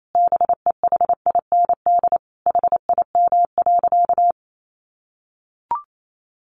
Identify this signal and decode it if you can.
Morse code, heard as a keyed tone, about 28 wpm, 700 Hz. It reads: BEHIND HIM.
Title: Sentences from Top 1000 Words 28wpm